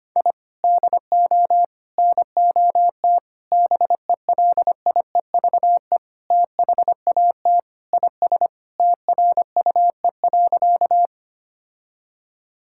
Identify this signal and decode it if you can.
Morse code, heard as a keyed tone, about 25 wpm, 700 Hz. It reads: I DO NOT BELIEVE THAT IS TRUE.
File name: practice / 25wpm / Sentences with 4-Word Phrases - 25wpm